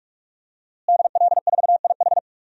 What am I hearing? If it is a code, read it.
Morse code, heard as a keyed tone, about 46 wpm, 700 Hz. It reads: DL4IH
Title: DXCC-100 Call Signs - 5 Characters - Repeated 3x - 45wpm